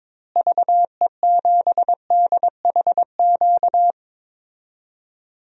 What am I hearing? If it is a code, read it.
Morse code, heard as a keyed tone, about 21 wpm, 700 Hz. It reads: VE7DHQ
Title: DXCC-100 Call Signs - 6 Characters - Repeated 3x - 22wpm